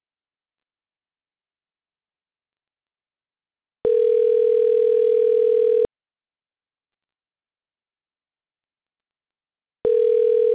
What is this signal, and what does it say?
Ringback tone (North America), 440 + 480 Hz.